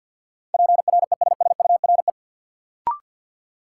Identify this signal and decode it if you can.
Morse code, heard as a keyed tone, about 52 wpm, 700 Hz. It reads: PRESSURE
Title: Sentences from Top 1400 Words - New Word Drill Alt - 50wpm